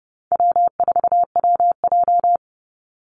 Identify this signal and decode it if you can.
Morse code, heard as a keyed tone, about 30 wpm, 700 Hz. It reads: W4WJ
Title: Easy Call Signs - 30wpm